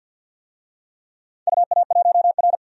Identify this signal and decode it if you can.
Morse code, heard as a keyed tone, about 49 wpm, 700 Hz. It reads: UA1R
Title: DXCC-100 Call Signs - 4 Characters - 50wpm